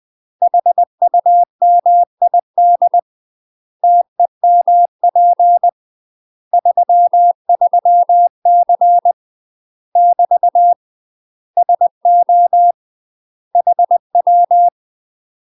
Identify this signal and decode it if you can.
Morse code, heard as a keyed tone, about 20 wpm, 700 Hz. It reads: HUMID TEMP 33C <BT> SO HW
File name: UK MASTER QSOs TRAINING / 20WPM / UK MASTER QSOs 0041